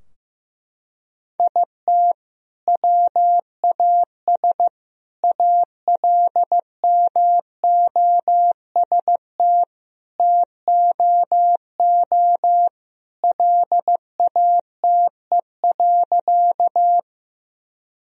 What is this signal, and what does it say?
Morse code, heard as a keyed tone, about 15 wpm, 700 Hz. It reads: IT WAS ALMOST TOO LATE.
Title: Sentences from Top 1000 Words - Review Part 4 - 15wpm